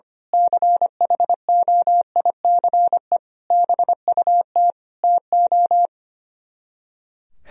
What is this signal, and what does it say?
Morse code, heard as a keyed tone, about 24 wpm, 700 Hz. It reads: CHOICE BUT TO
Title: Speed-Racing - Sets of 5 Words 25wpm